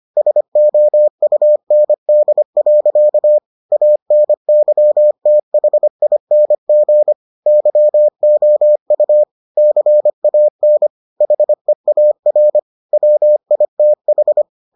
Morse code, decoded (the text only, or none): SOUND. ANYTHING YOU CAN HEAR WITH